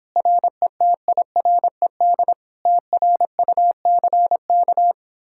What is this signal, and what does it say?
Morse code, heard as a keyed tone, about 31 wpm, 700 Hz. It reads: RETIRED TRUCK